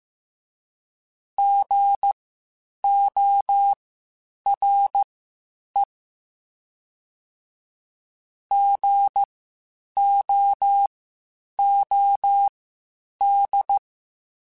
Morse code, heard as a keyed tone, about 15 wpm, spaced as for 8 wpm, 800 Hz. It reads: GORE GOOD
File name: Spaced Alphas / Spaced Alphas mp3s / Gs sp @13